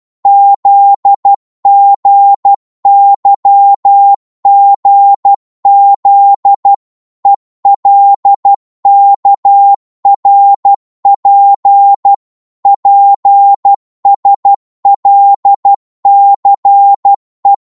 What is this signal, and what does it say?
Morse code, heard as a keyed tone, about 12 wpm, 800 Hz. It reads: ZGYGZ ELKRP PSLCE
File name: Morse / ltr-12wpm